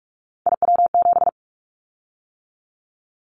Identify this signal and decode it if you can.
Morse code, heard as a keyed tone, about 44 wpm, 700 Hz. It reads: IW7
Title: Callsign Prefixes for Top 25 DXCC - 45wpm